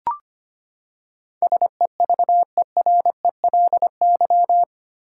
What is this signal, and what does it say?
Morse code, heard as a keyed tone, about 25 wpm, 700 Hz. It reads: SEVERELY
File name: Eight Letter Words - 25wpm